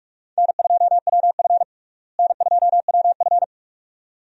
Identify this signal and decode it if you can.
Morse code, heard as a keyed tone, about 45 wpm, 700 Hz. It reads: N2WF N2WF
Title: USA Call Signs - 4 Characters - Repeated 2x - 45wpm